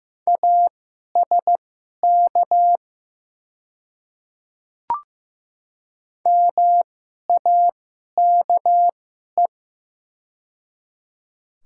Morse code, heard as a keyed tone, about 15 wpm, spaced as for 11 wpm, 700 Hz. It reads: ASK MAKE
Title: 2x Character Spacing - Top 500 Words - 15wpm